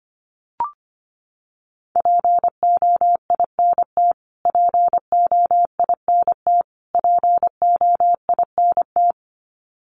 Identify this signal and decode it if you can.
Morse code, heard as a keyed tone, about 25 wpm, 700 Hz. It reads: POINT POINT POINT
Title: Top 500 Words Repeated 25wpm